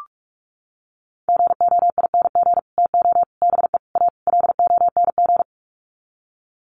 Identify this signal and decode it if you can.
Morse code, heard as a keyed tone, about 45 wpm, 700 Hz. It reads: GOING TO BE A LONG
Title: Sets of 5 Words Encore2 45wpm